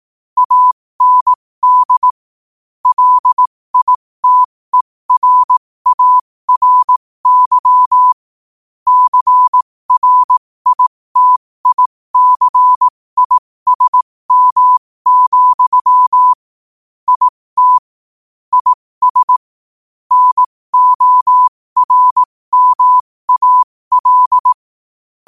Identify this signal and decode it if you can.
Morse code, heard as a keyed tone, about 18 wpm, spaced as for 15 wpm, 1000 Hz. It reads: AND LITERARY CRITICISM, IT IS NORMAL
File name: qotdfiles / 15wpm